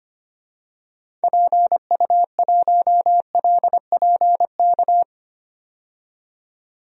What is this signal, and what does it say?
Morse code, heard as a keyed tone, about 25 wpm, 700 Hz. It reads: PU1LPK
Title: DXCC-100 Call Signs - 6 Characters - 25wpm